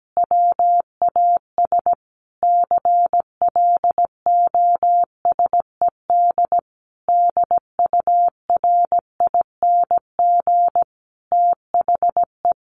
Morse code, heard as a keyed tone, about 17 wpm, 700 Hz. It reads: WAS CLOSED DURING THE
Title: Sentences from Top 1400 Words 17wpm